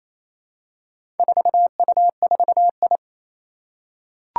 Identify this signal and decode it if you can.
Morse code, heard as a keyed tone, about 29 wpm, 700 Hz. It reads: 4U4I
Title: Four Challenging Letter-Number Combinations 28wpm